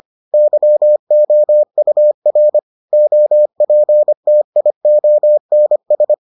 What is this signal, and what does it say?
Morse code, heard as a keyed tone, about 25 wpm, 600 Hz. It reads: YOUR OPTIONS